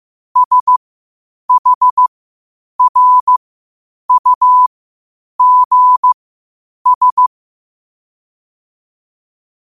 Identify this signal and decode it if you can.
Morse code, heard as a keyed tone, about 15 wpm, spaced as for 8 wpm, 1000 Hz. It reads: SHRUGS
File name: qotdfiles / 8wpm